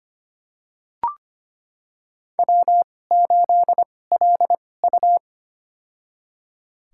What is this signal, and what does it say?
Morse code, heard as a keyed tone, about 25 wpm, spaced as for 18 wpm, 700 Hz. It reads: W8LU
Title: USA Call Signs - 4 Characters - 2x Character Spacing - 25wpm